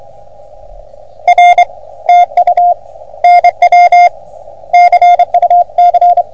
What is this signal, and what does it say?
Morse code, heard as a keyed tone, about 32 wpm, 700 Hz. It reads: R TU NW CUC
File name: OM3PA PILUP CUC 388